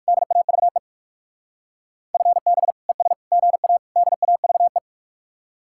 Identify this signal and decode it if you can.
Morse code, heard as a keyed tone, about 56 wpm, 700 Hz. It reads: DAVE FB ES GA DAVE
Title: Speed-Racing Common QSO Phrases - 45wpm